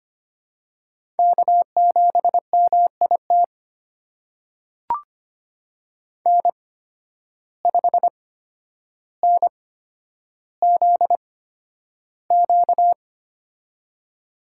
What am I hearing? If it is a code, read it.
Morse code, heard as a keyed tone, about 25 wpm, spaced as for 11 wpm, 700 Hz. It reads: K7MIT N5NZQ
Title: USA Call Signs - 5 Characters - 8x Character Spacing - 25wpm